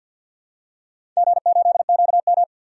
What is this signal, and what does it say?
Morse code, heard as a keyed tone, about 52 wpm, 700 Hz. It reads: G8QM
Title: DXCC-100 Call Signs - 4 Characters - 50wpm